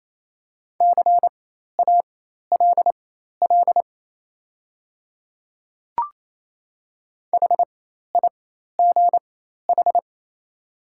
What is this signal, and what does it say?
Morse code, heard as a keyed tone, about 28 wpm, spaced as for 13 wpm, 700 Hz. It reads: CALL HIGH
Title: 4x Character Spacing - Top 500 Words - 28wpm